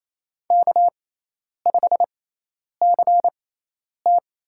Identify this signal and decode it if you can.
Morse code, heard as a keyed tone, about 27 wpm, spaced as for 10 wpm, 700 Hz. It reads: K5CT
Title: USA Call Signs - 6x Character Spacing - 28wpm